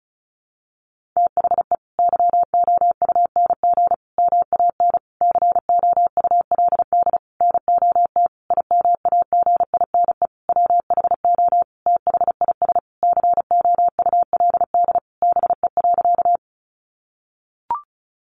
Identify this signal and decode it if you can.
Morse code, heard as a keyed tone, about 34 wpm, 700 Hz. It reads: THE YOUNG MAN COULD NOT IMAGINE WHO THIS COULD BE.
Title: Sentences from Top 1400 Words 35wpm